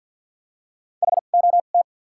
Morse code, heard as a keyed tone, about 53 wpm, spaced as for 37 wpm, 700 Hz. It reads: HOT
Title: 2x Character Spacing - Top 300 Words - 50wpm